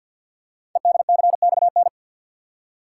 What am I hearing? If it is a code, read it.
Morse code, heard as a keyed tone, about 49 wpm, 700 Hz. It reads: EDCXN
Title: Code Groups - 5 Letters - 50wpm